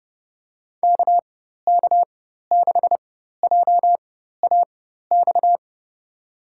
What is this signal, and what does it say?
Morse code, heard as a keyed tone, about 30 wpm, spaced as for 14 wpm, 700 Hz. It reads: KK6JAX
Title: USA Call Signs - 6 Characters - 4x Character Spacing - 30wpm